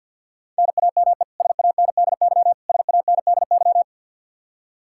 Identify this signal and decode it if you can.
Morse code, heard as a keyed tone, about 50 wpm, 700 Hz. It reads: NAME SANDY SANDY
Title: K1USN SST QSO Practice 50wpm